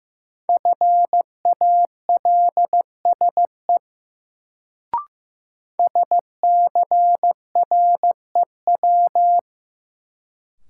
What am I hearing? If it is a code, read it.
Morse code, heard as a keyed tone, about 15 wpm, 700 Hz. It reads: FALSE SCREW
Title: Sentences from Top 1600 Words - New Word Drill Alt - 15wpm